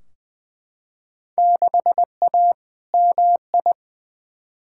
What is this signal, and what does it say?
Morse code, heard as a keyed tone, about 20 wpm, 700 Hz. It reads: 6A MI
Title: Field Day Exchanges - 20wpm